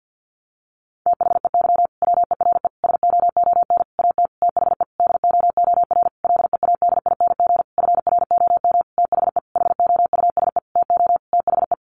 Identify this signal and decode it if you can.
Morse code, heard as a keyed tone, about 50 wpm, 700 Hz. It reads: THEY WERE SOON AT THE DOOR LEADING FROM THE HOUSE TO THE
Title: Sentences from Top 1200 Words 50wpm